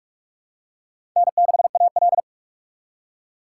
Morse code, heard as a keyed tone, about 47 wpm, 700 Hz. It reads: N6AL
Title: USA Call Signs - 2x Character Spacing - 45wpm